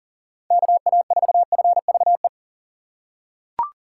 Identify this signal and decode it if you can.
Morse code, heard as a keyed tone, about 40 wpm, 700 Hz. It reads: KA4FVE
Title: USA Call Signs - 6 Characters - 2x Character Spacing - 40wpm